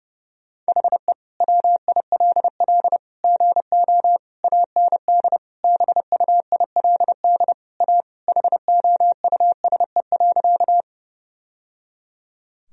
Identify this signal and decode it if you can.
Morse code, heard as a keyed tone, about 32 wpm, 700 Hz. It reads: HE WILL GO AND BUILD A HOUSE.
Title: Sentences from Top 1000 Words - Review Part 1 - 30wpm